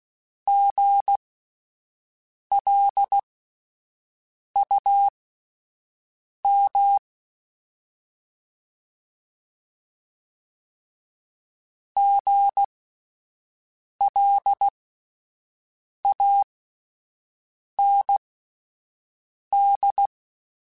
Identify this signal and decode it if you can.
Morse code, heard as a keyed tone, about 16 wpm, spaced as for 5 wpm, 800 Hz. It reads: GLUM GLAND
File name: Spaced Alphas / Spaced Alphas mp3s / Gs sp @10